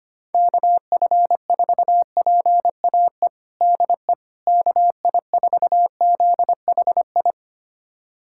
Morse code, heard as a keyed tone, about 25 wpm, 700 Hz. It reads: KF4PAE DE KI4ZHI
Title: USA Callsign DE Callsign - 6 Characters - 25wpm